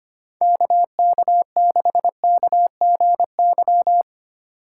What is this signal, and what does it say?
Morse code, heard as a keyed tone, about 32 wpm, 700 Hz. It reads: KK6KGY